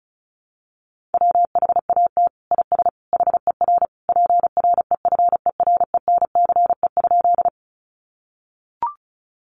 Morse code, heard as a keyed tone, about 34 wpm, 700 Hz. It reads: WHAT IS HER PREFERENCE?
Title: Sentences from Top 1500 Words 35wpm